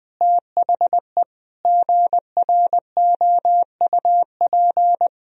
Morse code, heard as a keyed tone, about 21 wpm, 700 Hz. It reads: THE GROUP
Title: Sentences from Top 1000 Words - Review Part 1 - 20wpm